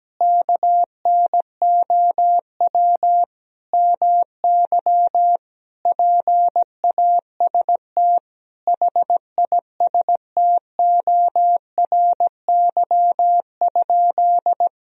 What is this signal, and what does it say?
Morse code, heard as a keyed tone, about 17 wpm, 700 Hz. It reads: KNOW MY PAST HISTORY?
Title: Sentences from Top 1000 Words - Review Part 3 - 17wpm